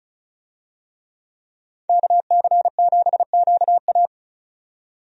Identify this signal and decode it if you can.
Morse code, heard as a keyed tone, about 34 wpm, 700 Hz. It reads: KC7QA
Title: USA Call Signs - 5 Characters - 35wpm